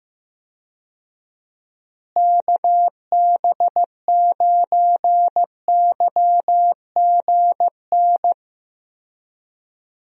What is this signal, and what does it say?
Morse code, heard as a keyed tone, about 15 wpm, 700 Hz. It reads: KB9YGN
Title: USA Call Signs - 6 Characters - Repeated 3x - 15wpm